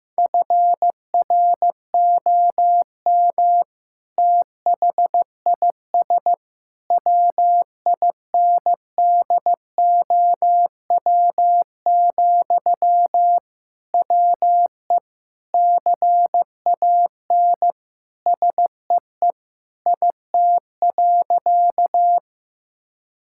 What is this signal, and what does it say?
Morse code, heard as a keyed tone, about 15 wpm, 700 Hz. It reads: FROM THIS WINDOW, WE CAN SEE IT.
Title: Sentences from Top 1600 Words 15wpm